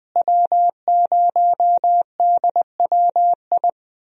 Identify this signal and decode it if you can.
Morse code, heard as a keyed tone, about 20 wpm, 700 Hz. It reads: W0DWI
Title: USA Call Signs - 5 Characters - Repeated 2x - 20wpm